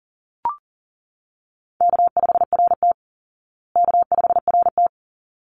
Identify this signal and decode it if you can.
Morse code, heard as a keyed tone, about 40 wpm, 700 Hz. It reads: K5RT K5RT
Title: Call Signs Repeated 3x - 40wpm